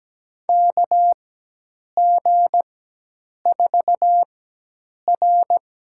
Morse code, heard as a keyed tone, about 17 wpm, spaced as for 8 wpm, 700 Hz. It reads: KG4R
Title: USA Call Signs - 6 Characters - 4x Character Spacing - 17wpm